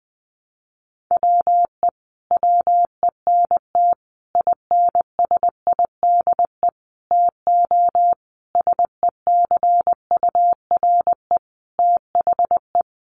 Morse code, heard as a keyed tone, about 19 wpm, 700 Hz. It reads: WE WENT INSIDE TO SECURE THE
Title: Sentences from Top 1200 Words 20wpm